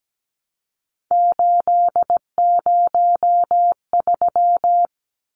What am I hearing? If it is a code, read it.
Morse code, heard as a keyed tone, about 17 wpm, 700 Hz. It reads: 803
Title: Numbers - 3-Digit 17wpm